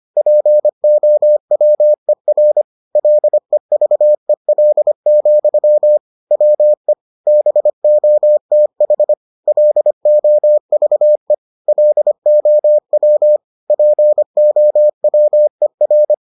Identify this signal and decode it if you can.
Morse code, heard as a keyed tone, about 25 wpm, 600 Hz. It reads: POWER LEVEL, WE BOTH LOVE LOW POWER